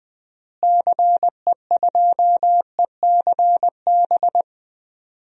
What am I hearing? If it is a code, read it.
Morse code, heard as a keyed tone, about 20 wpm, 700 Hz. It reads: CE2ECB
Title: DXCC-100 Call Signs - 6 Characters - Repeated 3x - 20wpm